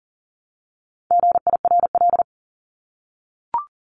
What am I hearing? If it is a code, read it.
Morse code, heard as a keyed tone, about 40 wpm, 700 Hz. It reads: GIRL
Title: Top 500 Words 40wpm